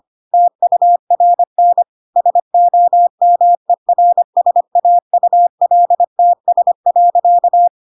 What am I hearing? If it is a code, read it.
Morse code, heard as a keyed tone, about 25 wpm, 700 Hz. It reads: TURN SOMERSAULTS.
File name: Pinocchio / Pinocchio ch2 25 25 ik1hge